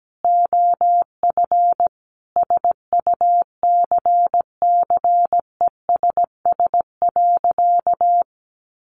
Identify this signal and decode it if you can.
Morse code, heard as a keyed tone, about 17 wpm, 700 Hz. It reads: OF SUCCESS.
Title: Sentences from Top 1300 Words 17wpm